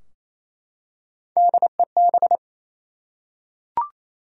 Morse code, heard as a keyed tone, about 28 wpm, 700 Hz. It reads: DEB
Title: USA Common Names - Max 3 Letters - 28wpm